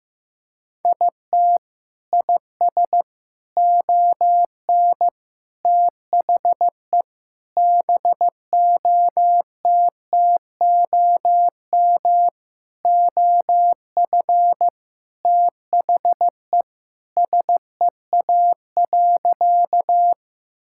Morse code, heard as a keyed tone, about 16 wpm, 700 Hz. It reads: IT IS ON THE BOTTOM OF THE SEA.
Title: Sentences from Top 1000 Words - Review Part 6 - 15wpm